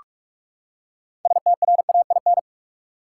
Morse code, heard as a keyed tone, about 46 wpm, 700 Hz. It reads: STRAIN
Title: Six Letter Words - 45wpm